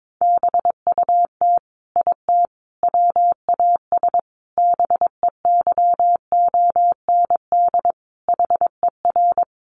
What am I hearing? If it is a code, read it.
Morse code, heard as a keyed tone, about 22 wpm, 700 Hz. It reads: BUT IT WAS BEYOND HER
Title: Sentences from Top 1200 Words 22wpm